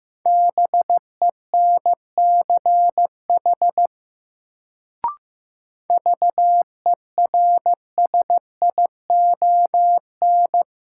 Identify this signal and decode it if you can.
Morse code, heard as a keyed tone, about 16 wpm, 700 Hz. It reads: BENCH VERSION
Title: Sentences from Top 1400 Words - New Word Drill - 15wpm